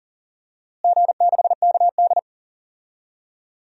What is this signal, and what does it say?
Morse code, heard as a keyed tone, about 40 wpm, 700 Hz. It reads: G6KD
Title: DXCC-100 Call Signs - 4 Characters - Repeated 3x - 40wpm